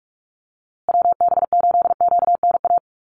Morse code, heard as a keyed tone, about 44 wpm, 700 Hz. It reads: WB8QNA
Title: USA Call Signs Repeated 3x 45wpm